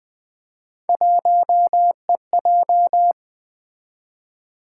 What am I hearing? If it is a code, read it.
Morse code, heard as a keyed tone, about 20 wpm, 700 Hz. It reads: 1EJ
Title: Three Letter-Number Combinations - 20wpm